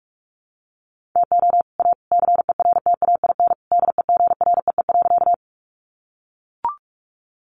Morse code, heard as a keyed tone, about 44 wpm, 700 Hz. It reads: TO A CERTAIN DEGREE.
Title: Sentences from Top 1500 Words 45wpm